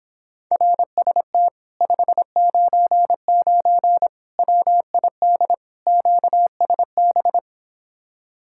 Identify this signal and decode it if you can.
Morse code, heard as a keyed tone, about 26 wpm, 700 Hz. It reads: RST 599 WID QSB